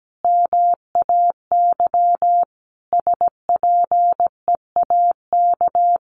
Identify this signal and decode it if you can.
Morse code, heard as a keyed tone, about 17 wpm, 700 Hz. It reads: MAY SPEAK
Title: Sentences from Top 1300 Words 17wpm